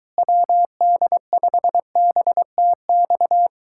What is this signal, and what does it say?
Morse code, heard as a keyed tone, about 23 wpm, 700 Hz. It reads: WD5BTX